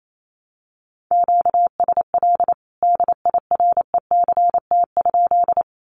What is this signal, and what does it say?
Morse code, heard as a keyed tone, about 28 wpm, 700 Hz. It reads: QSL DIRECT?
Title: Common QSO Phrases 28wpm